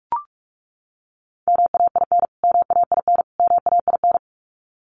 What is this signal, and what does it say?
Morse code, heard as a keyed tone, about 44 wpm, 700 Hz. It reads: MAIN MAIN MAIN
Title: Top 500 Words Repeated 45wpm